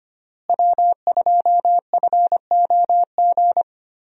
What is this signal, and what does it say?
Morse code, heard as a keyed tone, about 25 wpm, 700 Hz. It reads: W2FOG
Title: USA Call Signs - 5 Characters - Repeated 3x - 25wpm